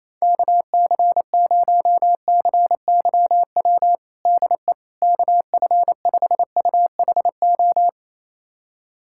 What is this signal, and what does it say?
Morse code, heard as a keyed tone, about 27 wpm, 700 Hz. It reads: KC0CYW DE KF5UHO
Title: USA Callsign DE Callsign - 6 Characters - 28wpm